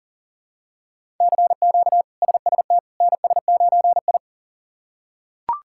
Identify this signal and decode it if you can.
Morse code, heard as a keyed tone, about 40 wpm, 700 Hz. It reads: CQ SST NS9I
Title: K1USN SST QSO Practice - 40wpm